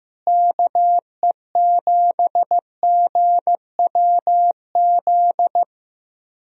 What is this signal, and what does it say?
Morse code, heard as a keyed tone, about 15 wpm, 700 Hz. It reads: KE7GWZ
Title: USA Call Signs - 6 Characters - Repeated 3x - 15wpm